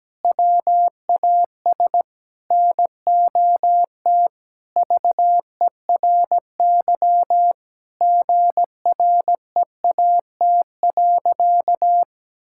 Morse code, heard as a keyed tone, about 17 wpm, 700 Hz. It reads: WAS NOT VERY GREAT.
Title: Sentences from Top 1000 Words - Review Part 5 - 17wpm